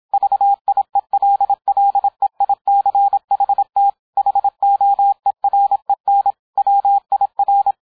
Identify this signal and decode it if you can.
Morse code, heard as a keyed tone, about 27 wpm, 800 Hz. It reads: VIELLEICHT HOEREN WIR